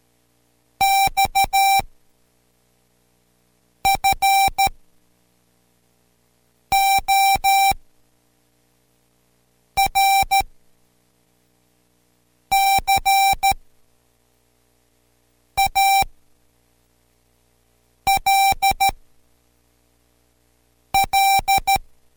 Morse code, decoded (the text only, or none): X FOR CALL